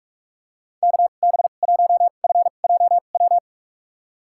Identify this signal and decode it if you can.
Morse code, heard as a keyed tone, about 45 wpm, spaced as for 32 wpm, 700 Hz. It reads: KB1FJW
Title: USA Call Signs - 6 Characters - 2x Character Spacing - 45wpm